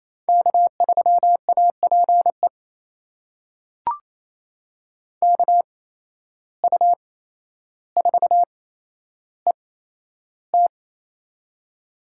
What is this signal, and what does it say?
Morse code, heard as a keyed tone, about 27 wpm, spaced as for 12 wpm, 700 Hz. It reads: K3APE KU4ET
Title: USA Call Signs - 5 Characters - 8x Character Spacing - 28wpm